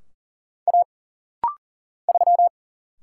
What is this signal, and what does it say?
Morse code, heard as a keyed tone, about 40 wpm, 700 Hz. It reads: A 3
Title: Single Letter-Number - Rapid-Fire 40wpm